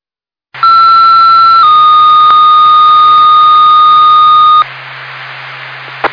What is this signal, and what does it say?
Two-tone page, heard as a one-second tone, then a three-second tone, about 1340 then 1230 Hz. It is heard from 0.6 to 4.6 s.